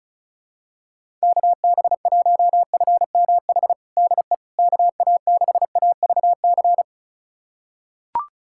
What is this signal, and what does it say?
Morse code, heard as a keyed tone, about 36 wpm, 700 Hz. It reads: KB1FMH DE KA6AVC
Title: USA Callsign DE Callsign - 6 Characters - 35wpm